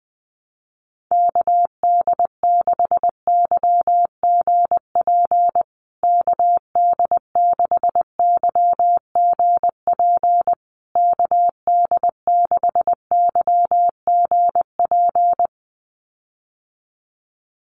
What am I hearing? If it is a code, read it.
Morse code, heard as a keyed tone, about 20 wpm, 700 Hz. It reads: KD6YGP KD6YGP KD6YGP
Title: USA Call Signs Repeated 3x 20wpm